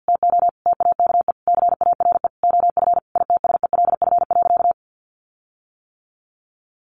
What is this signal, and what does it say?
Morse code, heard as a keyed tone, about 50 wpm, 700 Hz. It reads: TO TAKE CARE OF ITSELF.
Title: Sentences from Top 1500 Words 50wpm